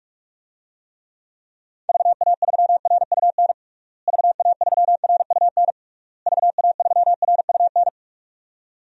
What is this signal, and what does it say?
Morse code, heard as a keyed tone, about 46 wpm, 700 Hz. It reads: VA3RUN VA3RUN VA3RUN
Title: DXCC-100 Call Signs - 6 Characters - Repeated 3x - 45wpm